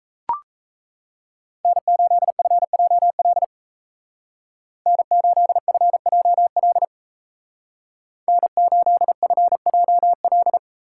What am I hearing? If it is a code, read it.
Morse code, heard as a keyed tone, about 33 wpm, 700 Hz. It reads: N8FJL N8FJL N8FJL
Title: Speed Racing - USA 1x3 Call Signs 28wpm